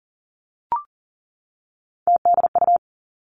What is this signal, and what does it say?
Morse code, heard as a keyed tone, about 42 wpm, 700 Hz. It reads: TDU
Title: Three Challenging Letter-Number Combinations 40wpm